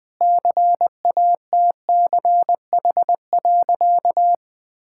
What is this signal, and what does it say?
Morse code, heard as a keyed tone, about 21 wpm, 700 Hz. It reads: CATCH.
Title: Sentences with 2-Word Phrases - 20wpm